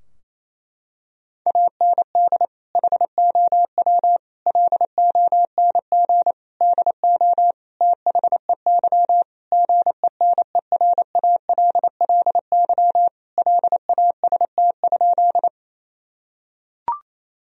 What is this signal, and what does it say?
Morse code, heard as a keyed tone, about 28 wpm, 700 Hz. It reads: AND HOW LONG DO THEY GENERALLY LAST?
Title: Sentences from Top 1000 Words - Review Part 4 - 28wpm